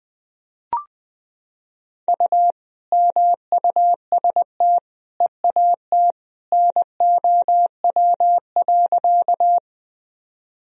Morse code, heard as a keyed tone, about 21 wpm, 700 Hz. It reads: U MUST EAT NOW.
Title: Sentences from Top 300 Words and Abbreviations 20wpm